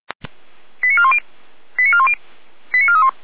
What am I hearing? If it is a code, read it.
Five-tone selective call (ZVEI), heard as quick runs of five tones, tones of about 70 ms. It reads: 89410 89410 89431
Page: ZVEI